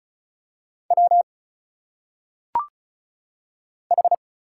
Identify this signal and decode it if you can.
Morse code, heard as a keyed tone, about 36 wpm, 700 Hz. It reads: W H
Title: Single Letters - Phonetic Pronunciation 35wpm